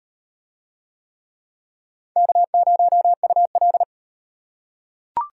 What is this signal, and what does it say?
Morse code, heard as a keyed tone, about 33 wpm, 700 Hz. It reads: K0UL